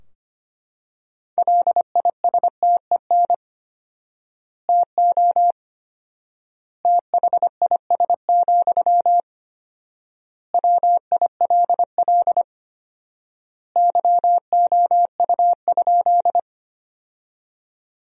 Morse code, heard as a keyed tone, about 25 wpm, 700 Hz. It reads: LISTEN TO THIS, WILL YOU?
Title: 4x Word-Spacing - Sentences from Top 400 Words - 25wpm